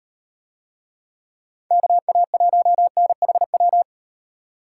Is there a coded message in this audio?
Yes: Morse code from 1.7 to 3.8 s, heard as a keyed tone, about 38 wpm, 700 Hz. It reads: KA1NHW